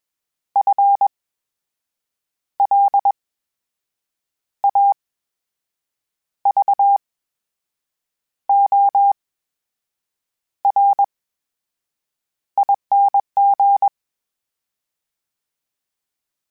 Morse code, heard as a keyed tone, about 21 wpm, spaced as for 6 wpm, 800 Hz. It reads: FLAVORING
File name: Special INGs-mp3s / Special INGs-mp3s / VerySlowToFaster Ing @20